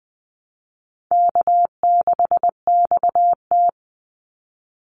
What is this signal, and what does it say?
Morse code, heard as a keyed tone, about 20 wpm, 700 Hz. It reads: K6XT
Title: Easy Call Signs Repeated 2x - 20wpm